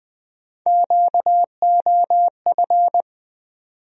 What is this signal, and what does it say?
Morse code, heard as a keyed tone, about 20 wpm, 700 Hz. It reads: QOF
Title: USA Call Sign Postfixes - 3 Letters 20wpm